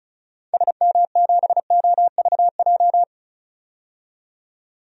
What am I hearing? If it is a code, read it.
Morse code, heard as a keyed tone, about 36 wpm, 700 Hz. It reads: SM7OVJ
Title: DXCC-100 Call Signs - 6 Characters - 35wpm